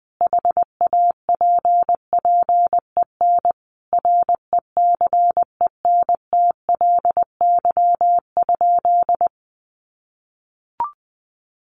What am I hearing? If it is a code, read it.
Morse code, heard as a keyed tone, about 20 wpm, 700 Hz. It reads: HAPPEN RECENTLY?
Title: Sentences from Top 1200 Words 20wpm